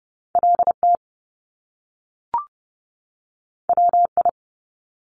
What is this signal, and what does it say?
Morse code, heard as a keyed tone, about 30 wpm, 700 Hz. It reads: LT WI
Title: Top Two Letter Combinations in Words 30wpm